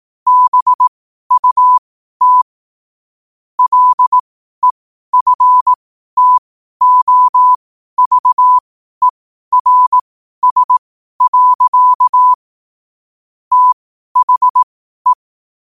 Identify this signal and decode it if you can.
Morse code, heard as a keyed tone, about 18 wpm, spaced as for 12 wpm, 1000 Hz. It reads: BUT LEFTOVERS. THE